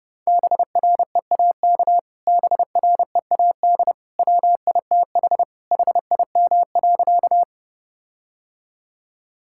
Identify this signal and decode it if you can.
Morse code, heard as a keyed tone, about 31 wpm, 700 Hz. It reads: BREAK BREAD WITH HIM.
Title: Sentences from Top 1600 Words 30wpm